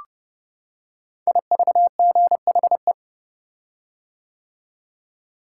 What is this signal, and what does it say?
Morse code, heard as a keyed tone, about 30 wpm, 700 Hz. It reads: IVGHE
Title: Code Groups - 5 Letters - 30wpm